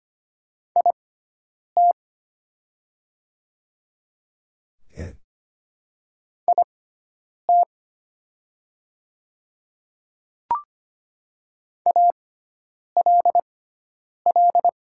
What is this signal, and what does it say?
Morse code, heard as a keyed tone, about 25 wpm, spaced as for 9 wpm, 700 Hz. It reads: IT IT ALL
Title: 6x Character Spacing - Top 300 Words - 25wpm